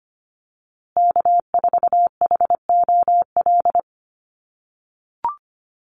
Morse code, heard as a keyed tone, about 25 wpm, 700 Hz. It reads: K4HOL
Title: USA Call Signs Repeated 3x 25wpm